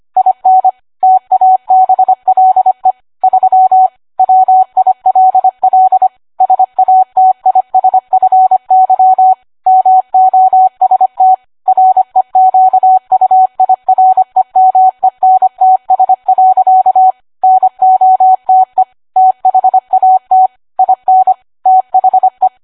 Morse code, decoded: IN TABLE 3 WILL SATISFY MOST REQUIREMENTS. NOTE THAT IN THE